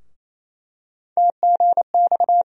Morse code, heard as a keyed tone, about 28 wpm, 700 Hz. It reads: TGX
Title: USA Call Sign Postfixes - 3 Letters 28wpm